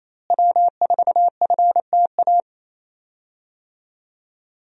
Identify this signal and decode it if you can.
Morse code, heard as a keyed tone, about 28 wpm, 700 Hz. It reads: W4FTA
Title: USA Call Signs - 5 Characters - Repeated 2x - 28wpm